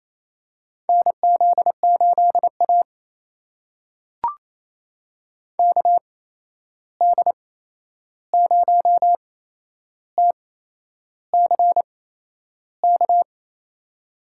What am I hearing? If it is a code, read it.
Morse code, heard as a keyed tone, about 28 wpm, spaced as for 11 wpm, 700 Hz. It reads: NZ8A KD0TCK
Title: USA Call Signs - 8x Character Spacing - 28wpm